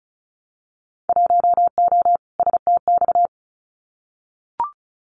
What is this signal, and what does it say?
Morse code, heard as a keyed tone, about 34 wpm, 700 Hz. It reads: 1O STX
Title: Winter Field Day Exchanges 35wpm